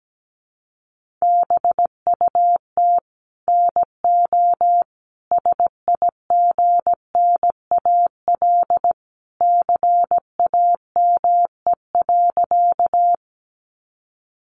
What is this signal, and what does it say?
Morse code, heard as a keyed tone, about 17 wpm, 700 Hz. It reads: BUT NO SIGNAL CAME.
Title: Sentences from Top 1400 Words 17wpm